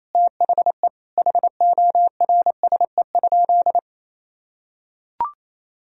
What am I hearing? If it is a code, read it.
Morse code, heard as a keyed tone, about 28 wpm, 700 Hz. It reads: THE HORSE?
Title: Sentences from Top 500 Words and Abbreviations 28wpm